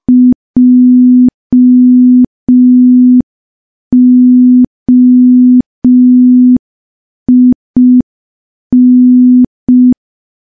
Morse code, heard as a keyed tone, about 5 wpm, 250 Hz. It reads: JOIN
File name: morse / morse-training-05WPM